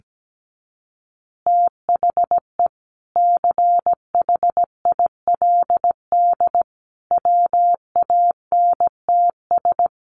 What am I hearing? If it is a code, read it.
Morse code, heard as a keyed tone, about 17 wpm, 700 Hz. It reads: THE CHILD WANTS
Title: Additional Easy Phrases 17wpm